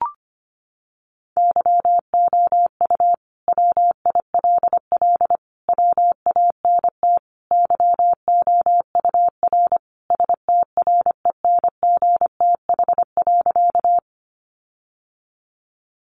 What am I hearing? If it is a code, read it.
Morse code, heard as a keyed tone, about 25 wpm, 700 Hz. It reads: YOU WILL WANT YOUR STRENGTH.
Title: Sentences from Top 1300 Words 25wpm